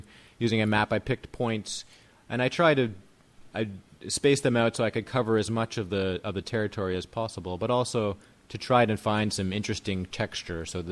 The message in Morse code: none

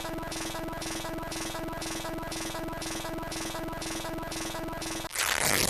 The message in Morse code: none

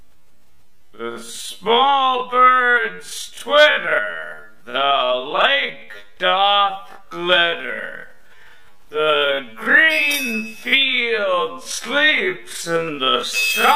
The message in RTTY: none

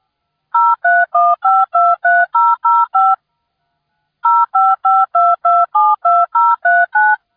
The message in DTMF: 031523005 0552272039

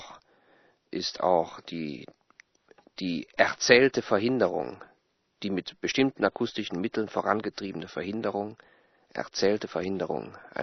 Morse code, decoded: none

